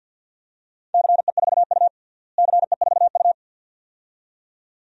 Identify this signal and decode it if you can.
Morse code, heard as a keyed tone, about 50 wpm, 700 Hz. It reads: CE4U CE4U
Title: DXCC-100 Call Signs - 4 Characters - Repeated 2x - 50wpm